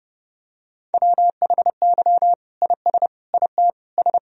Morse code, heard as a keyed tone, about 32 wpm, 700 Hz. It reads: WHY IS IT S
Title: Sentences from Top 1000 Words - Review Part 1 - 30wpm